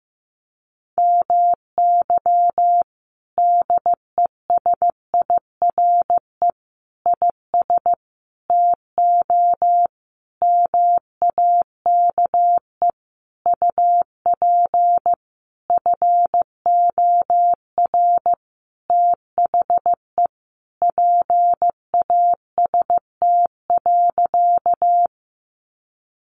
Morse code, heard as a keyed tone, about 15 wpm, 700 Hz. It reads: MY DESIRE IS TO MAKE UP FOR THE PAST.
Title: Sentences from Top 1100 Words 15wpm